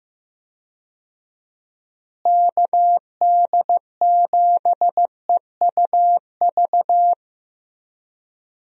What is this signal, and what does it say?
Morse code, heard as a keyed tone, about 15 wpm, 700 Hz. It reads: KD7EUV
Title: USA Call Signs - 6 Characters - Repeated 3x - 15wpm